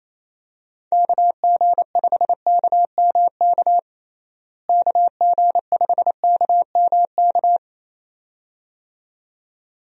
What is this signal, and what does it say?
Morse code, heard as a keyed tone, about 28 wpm, 700 Hz. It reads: KG5KMK KG5KMK
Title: USA Call Signs - 6 Characters - Repeated 2x - 28wpm